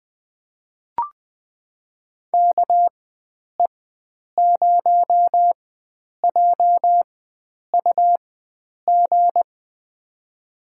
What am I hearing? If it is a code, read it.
Morse code, heard as a keyed tone, about 20 wpm, spaced as for 9 wpm, 700 Hz. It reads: KE0JUG
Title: USA Call Signs - 4x Character Spacing - 20wpm